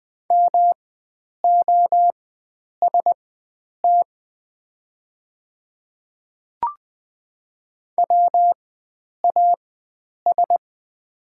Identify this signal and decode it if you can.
Morse code, heard as a keyed tone, about 20 wpm, spaced as for 9 wpm, 690 Hz. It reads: MOST WAS
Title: 4x Character Spacing - Top 100 Words - 20wpm